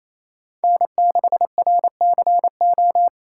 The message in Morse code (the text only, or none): N6RCO